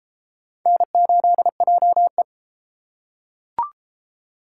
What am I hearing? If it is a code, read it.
Morse code, heard as a keyed tone, about 31 wpm, 700 Hz. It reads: N8JE